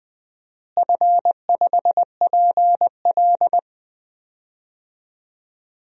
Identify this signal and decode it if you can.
Morse code, heard as a keyed tone, about 20 wpm, 700 Hz. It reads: F5PL
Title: DXCC-100 Call Signs - 4 Characters - Repeated 2x - 20wpm